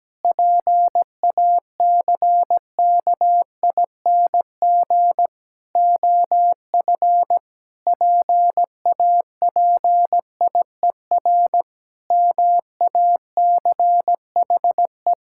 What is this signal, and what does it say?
Morse code, heard as a keyed tone, about 17 wpm, 700 Hz. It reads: PACKING OF PAPIER MACHE
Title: Pangram Challenge v2 - 17wpm